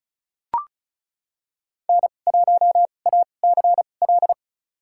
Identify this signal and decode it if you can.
Morse code, heard as a keyed tone, about 36 wpm, spaced as for 25 wpm, 700 Hz. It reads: N1ACL
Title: USA Call Signs - 2x Character Spacing - 35wpm